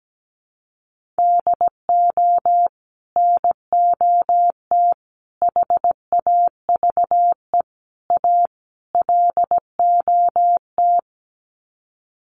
Morse code, heard as a keyed tone, about 17 wpm, 700 Hz. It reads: DO NOT HAVE A LOT
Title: Sets of 5 Words Encore 17wpm